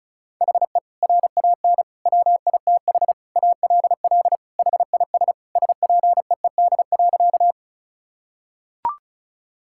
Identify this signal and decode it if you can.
Morse code, heard as a keyed tone, about 36 wpm, 700 Hz. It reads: HE RAN WITH ALL HIS SPEED.
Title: Sentences from Top 1000 Words - Review Part 5 - 35wpm